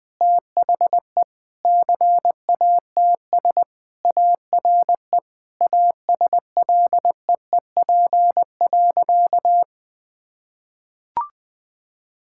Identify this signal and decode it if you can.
Morse code, heard as a keyed tone, about 21 wpm, 700 Hz. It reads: THE CATS ARE ASLEEP.
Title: Sentences with 3-Word Phrases - 20wpm